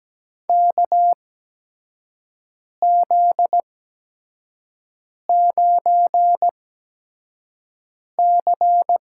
Morse code, heard as a keyed tone, about 17 wpm, spaced as for 5 wpm, 700 Hz. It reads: KZ9C
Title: USA Call Signs - 4 Characters - 8x Character Spacing - 17wpm